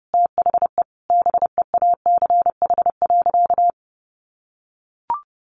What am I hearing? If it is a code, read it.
Morse code, heard as a keyed tone, about 30 wpm, 700 Hz. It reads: THE BEACH.
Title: Sentences from Top 1500 Words 30wpm